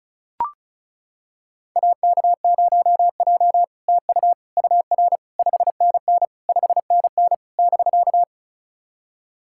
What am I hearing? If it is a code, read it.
Morse code, heard as a keyed tone, about 36 wpm, 700 Hz. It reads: AK0J TU UR 5NN 5NN <BK>
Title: SOTA QSO Practice 35wpm